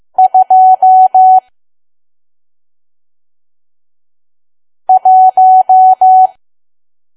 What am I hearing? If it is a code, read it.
Morse code, heard as a keyed tone, about 15 wpm, 750 Hz. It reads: 2 1